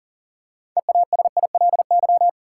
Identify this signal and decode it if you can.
Morse code, heard as a keyed tone, about 40 wpm, 700 Hz. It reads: EASILY
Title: Six Letter Words - 40wpm